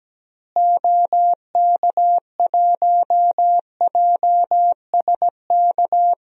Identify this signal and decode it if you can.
Morse code, heard as a keyed tone, about 17 wpm, 700 Hz. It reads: OK1JSK
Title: DXCC-100 Call Signs - 6 Characters - Repeated 2x - 17wpm